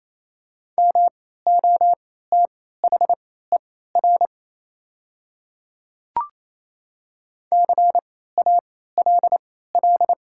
Morse code, heard as a keyed tone, about 28 wpm, spaced as for 16 wpm, 690 Hz. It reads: MOTHER CALL
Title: 3x Character Spacing - Top 200 Words - 28wpm